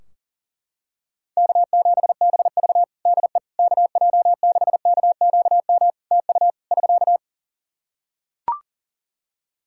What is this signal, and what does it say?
Morse code, heard as a keyed tone, about 40 wpm, 700 Hz. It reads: K7BV DE KJ6KQM TU <SK>
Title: Practice QSO - Exchange 4 - Ending - 40wpm